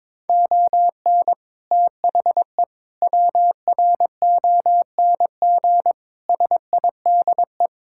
Morse code, heard as a keyed tone, about 22 wpm, 700 Hz. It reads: ON THE WRONG SIDE
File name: practice / 22wpm / Sets of 6 Words - 22wpm